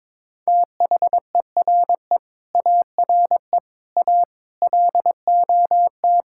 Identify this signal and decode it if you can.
Morse code, heard as a keyed tone, about 22 wpm, 700 Hz. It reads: THERE ARE A LOT
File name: practice / 22wpm / Sentences with 4-Word Phrases - 22wpm